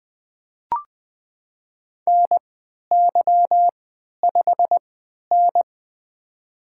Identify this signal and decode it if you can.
Morse code, heard as a keyed tone, about 20 wpm, spaced as for 11 wpm, 700 Hz. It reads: NY5N
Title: USA Call Signs - 3x Character Spacing - 20wpm